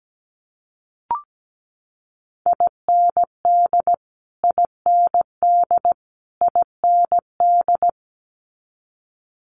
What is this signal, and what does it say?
Morse code, heard as a keyed tone, about 17 wpm, 700 Hz. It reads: IND IND IND
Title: Multi-letter Phonemes Repeated 3x 17wpm